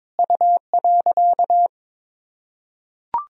Morse code, heard as a keyed tone, about 21 wpm, 700 Hz. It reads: U.
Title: Sentences from Top 300 Words and Abbreviations 22wpm